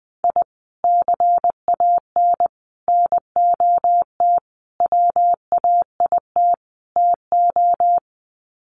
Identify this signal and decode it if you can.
Morse code, heard as a keyed tone, about 19 wpm, 700 Hz. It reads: I CAN NOT WAIT TO
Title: Sets of 5 Words Encore 20wpm